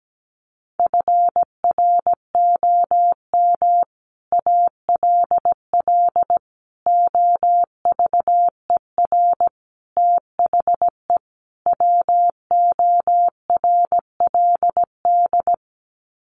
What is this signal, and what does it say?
Morse code, heard as a keyed tone, about 17 wpm, 700 Hz. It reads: FROM ALL OVER THE WORLD
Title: Sets of 5 Words 17wpm